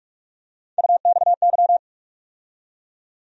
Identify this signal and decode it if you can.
Morse code, heard as a keyed tone, about 46 wpm, 700 Hz. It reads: UXY
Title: USA Call Sign Postfixes - 3 Letters 45wpm